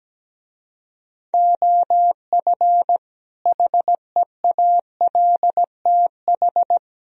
Morse code, heard as a keyed tone, about 17 wpm, 700 Hz. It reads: OF HEALTH
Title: Speed-Racing - Sets of 5 Words 17wpm